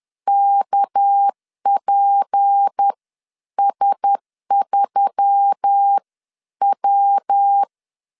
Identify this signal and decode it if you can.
Morse code, heard as a keyed tone, about 10 wpm, spaced as for 9 wpm, 800 Hz. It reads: KPS3W
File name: CW / DET / 1 6 1500C S B 10wpm 4137